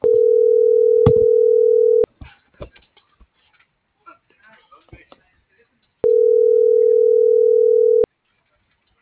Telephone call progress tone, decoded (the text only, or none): ringback tone